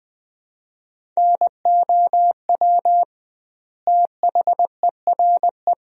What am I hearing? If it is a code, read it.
Morse code, heard as a keyed tone, about 20 wpm, 700 Hz. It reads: NOW THERE
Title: 2x Word-Spacing - Sentences from Top 100 Words - 20wpm